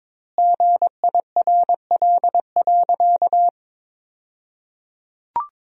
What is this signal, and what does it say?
Morse code, heard as a keyed tone, about 22 wpm, 700 Hz. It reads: GIRL.
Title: 2x Word-Spacing - Sentences from Top 400 Words - 22wpm